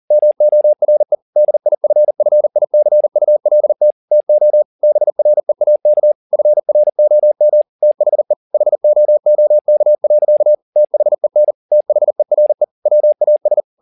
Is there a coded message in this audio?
Yes: Morse code from 0.1 to 13.6 s, heard as a keyed tone, about 40 wpm, 600 Hz. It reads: MORE DIFFICULT TO BREAK FROM THE HOOK. THEN THERE WAS